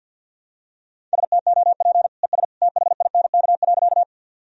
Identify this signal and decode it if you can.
Morse code, heard as a keyed tone, about 47 wpm, 700 Hz. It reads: STOP ES THINK.
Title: Sentences from Top 300 Words and Abbreviations 50wpm